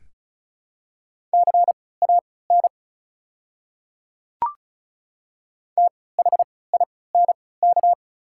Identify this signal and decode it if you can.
Morse code, heard as a keyed tone, about 34 wpm, spaced as for 20 wpm, 700 Hz. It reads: CAN THINK
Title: 3x Character Spacing - Top 200 Words - 35wpm